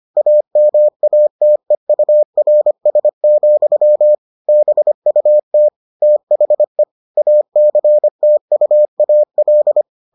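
Morse code, heard as a keyed tone, about 25 wpm, 600 Hz. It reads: AMATEURS, BUT THE ACTUAL
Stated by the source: Morse edition of the Foundations of Amateur Radio episodes.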